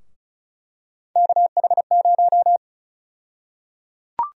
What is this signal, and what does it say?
Morse code, heard as a keyed tone, about 36 wpm, 700 Hz. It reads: KH0
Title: USA Call Sign Prefixes - 35wpm